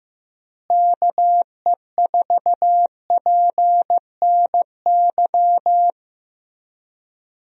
Morse code, heard as a keyed tone, about 15 wpm, 700 Hz. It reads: KE4PNY
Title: USA Call Signs - 6 Characters - Repeated 3x - 15wpm